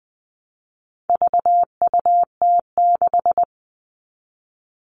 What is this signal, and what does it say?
Morse code, heard as a keyed tone, about 20 wpm, 700 Hz. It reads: VUT6
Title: All Cut Numbers - 4-Digit 20wpm